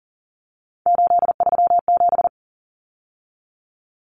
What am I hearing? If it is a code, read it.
Morse code, heard as a keyed tone, about 40 wpm, 700 Hz. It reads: 837
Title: Numbers - 3-Digit 40wpm